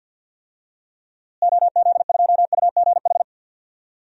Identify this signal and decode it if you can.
Morse code, heard as a keyed tone, about 52 wpm, 700 Hz. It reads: OZ2UGH
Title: DXCC-100 Call Signs - 6 Characters - Repeated 3x - 50wpm